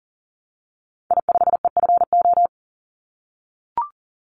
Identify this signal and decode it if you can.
Morse code, heard as a keyed tone, about 40 wpm, 700 Hz. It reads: I5EFO
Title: Call Signs - 40wpm